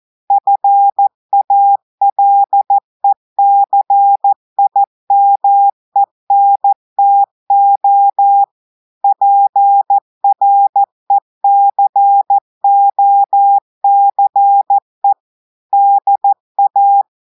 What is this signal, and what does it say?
Morse code, heard as a keyed tone, about 14 wpm, 800 Hz. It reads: FALECIMENTO PRECOCE DA
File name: audios / portugais / portugais texte en clair 1 vitesse 14 mots minutes